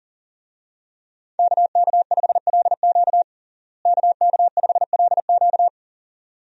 Morse code, heard as a keyed tone, about 40 wpm, 700 Hz. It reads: KK5LQ KK5LQ
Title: USA Call Signs - 5 Characters - Repeated 2x - 40wpm